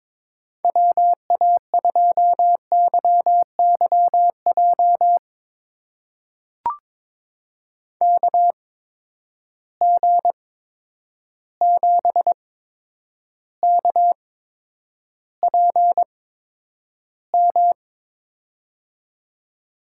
Morse code, heard as a keyed tone, about 21 wpm, spaced as for 9 wpm, 700 Hz. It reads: WA2YYJ KG7KPM
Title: USA Call Signs - 6 Characters - 8x Character Spacing - 22wpm